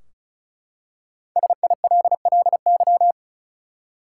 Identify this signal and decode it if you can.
Morse code, heard as a keyed tone, about 36 wpm, 700 Hz. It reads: SILLY
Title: Five Letter Words - 35wpm